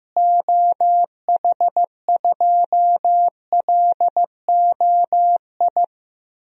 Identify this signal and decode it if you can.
Morse code, heard as a keyed tone, about 15 wpm, 700 Hz. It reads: OH2LOI
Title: DXCC-100 Call Signs - 6 Characters - Repeated 3x - 15wpm